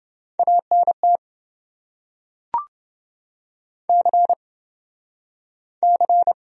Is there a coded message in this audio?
Yes: Morse code from 0.4 to 6.3 s, heard as a keyed tone, about 27 wpm, 700 Hz. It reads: ANT C C